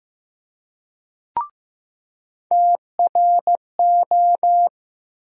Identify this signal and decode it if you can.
Morse code, heard as a keyed tone, about 15 wpm, 700 Hz. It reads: TRO
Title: USA Call Sign Postfixes - 3 Letters 15wpm